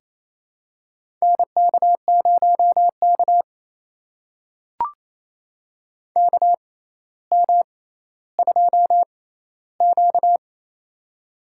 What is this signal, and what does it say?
Morse code, heard as a keyed tone, about 28 wpm, spaced as for 14 wpm, 700 Hz. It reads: NK0K KM2Q
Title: USA Call Signs - 4 Characters - 6x Character Spacing - 28wpm